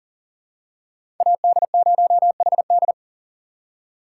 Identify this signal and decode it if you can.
Morse code, heard as a keyed tone, about 40 wpm, 700 Hz. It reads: AD0HD
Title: USA Call Signs - 5 Characters - 3x Character Spacing - 40wpm